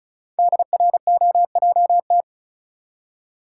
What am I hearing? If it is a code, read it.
Morse code, heard as a keyed tone, about 34 wpm, 700 Hz. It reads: DROJT
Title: Code Groups - 5 Letters - 35wpm